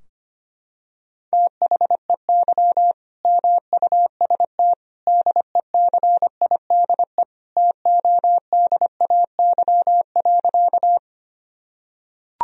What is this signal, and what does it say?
Morse code, heard as a keyed tone, about 25 wpm, 700 Hz. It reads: THEY MUST DECIDE TODAY.
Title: Sentences from Top 1000 Words - Review Part 3 - 25wpm